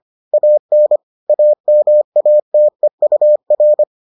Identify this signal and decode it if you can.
Morse code, heard as a keyed tone, about 25 wpm, 600 Hz. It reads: AN AMATEUR